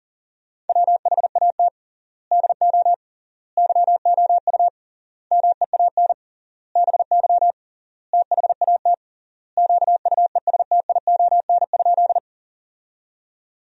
Morse code, heard as a keyed tone, about 40 wpm, 700 Hz. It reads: WHAT DO YOU MEAN BY THAT QUESTION?
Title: 3x Word-Spacing - Sentences from Top 400 Words - 40wpm